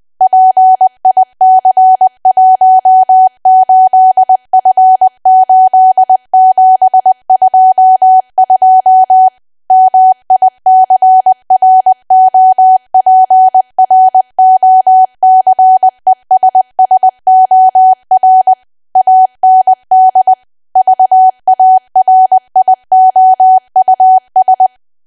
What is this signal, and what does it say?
Morse code, heard as a keyed tone, about 20 wpm, 750 Hz. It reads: PIC18F8722 MICROPROCESSOR AND VARIOUS
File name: cwa / W1AW-Level3 / 20WPM